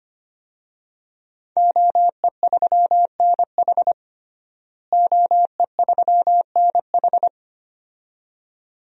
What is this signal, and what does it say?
Morse code, heard as a keyed tone, about 25 wpm, 700 Hz. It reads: OE3NH OE3NH
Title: DXCC-100 Call Signs - 5 Characters - Repeated 2x - 25wpm